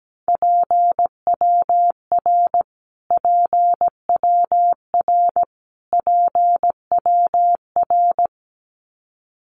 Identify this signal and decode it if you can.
Morse code, heard as a keyed tone, about 17 wpm, 700 Hz. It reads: PWR PWR PWR
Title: Abbreviations Repeated 3x - 17wpm